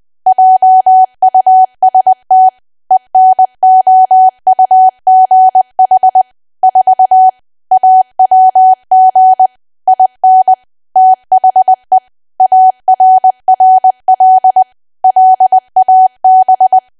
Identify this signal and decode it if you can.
Morse code, heard as a keyed tone, about 20 wpm, 750 Hz. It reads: JUST ENOUGH 4 AWG IN THE ARRL LAB
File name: cwa / W1AW-Level3 / 20WPM